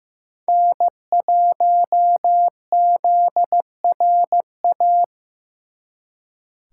Morse code, heard as a keyed tone, about 18 wpm, 700 Hz. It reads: N1ZRA
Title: Speed Racing - USA 1x3 Call Signs 15wpm